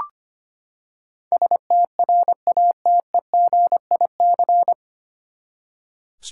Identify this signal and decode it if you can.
Morse code, heard as a keyed tone, about 25 wpm, 700 Hz. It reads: STRATEGIC
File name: Nine Letter Words - 25wpm